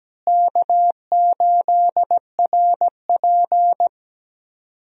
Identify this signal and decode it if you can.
Morse code, heard as a keyed tone, about 17 wpm, 700 Hz. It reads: K8RP
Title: USA Call Signs - 4 Characters - Repeated 3x - 17wpm